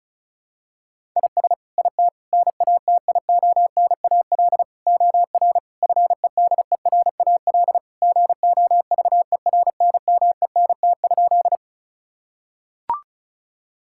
Morse code, heard as a keyed tone, about 36 wpm, 700 Hz. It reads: IS IT NATIONAL OR FEDERAL GOVERNMENT?
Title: Sentences from Top 1000 Words - Review Part 5 - 35wpm